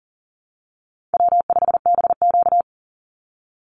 Morse code, heard as a keyed tone, about 40 wpm, 700 Hz. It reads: W5BQ
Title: Easy Call Signs Repeated 3x - 40wpm